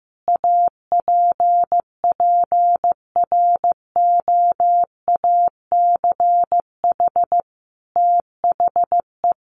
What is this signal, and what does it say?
Morse code, heard as a keyed tone, about 15 wpm, 700 Hz. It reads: APPROACH THE
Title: Sentences from Top 1400 Words 15wpm